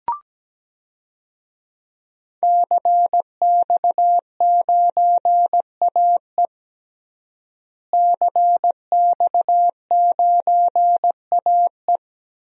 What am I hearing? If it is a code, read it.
Morse code, heard as a keyed tone, about 17 wpm, 700 Hz. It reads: CX9AE CX9AE
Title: DXCC-100 Call Signs - 5 Characters - Repeated 2x - 17wpm